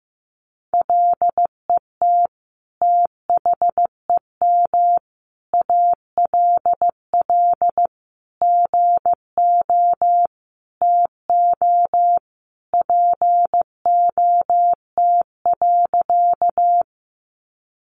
Morse code, heard as a keyed tone, about 14 wpm, 700 Hz. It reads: LET THEM ALL GO TO POT.
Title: Sentences from Top 1000 Words 15wpm